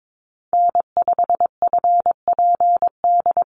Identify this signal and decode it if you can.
Morse code, heard as a keyed tone, about 22 wpm, 700 Hz. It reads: N5FPD
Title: USA Call Signs Repeated 3x 22wpm